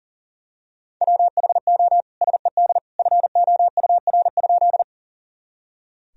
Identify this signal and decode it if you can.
Morse code, heard as a keyed tone, about 37 wpm, 700 Hz. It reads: WHO SED FOUR?
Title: Sentences from Top 300 Words and Abbreviations 40wpm